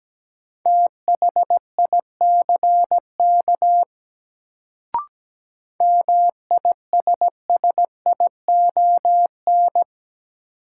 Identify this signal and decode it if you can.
Morse code, heard as a keyed tone, about 17 wpm, 700 Hz. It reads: THICK MISSION
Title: Sentences from Top 1600 Words - New Word Drill Alt - 17wpm